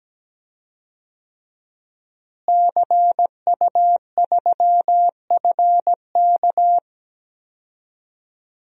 Morse code, heard as a keyed tone, about 17 wpm, 700 Hz. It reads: CU3FK
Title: DXCC-100 Call Signs - 5 Characters - Repeated 3x - 17wpm